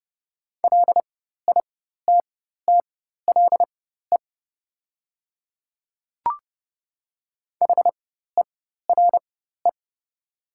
Morse code, heard as a keyed tone, about 30 wpm, spaced as for 14 wpm, 700 Hz. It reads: LITTLE HERE
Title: 4x Character Spacing - Top 200 Words - 30wpm